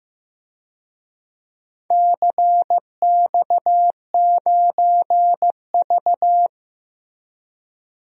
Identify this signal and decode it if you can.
Morse code, heard as a keyed tone, about 15 wpm, 700 Hz. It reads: CX9V
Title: DXCC-100 Call Signs - 4 Characters - Repeated 3x - 15wpm